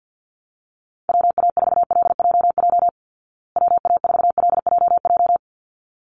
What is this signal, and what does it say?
Morse code, heard as a keyed tone, about 50 wpm, 700 Hz. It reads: WA4LJJ WA4LJJ
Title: Call Signs Repeated 3x - 50wpm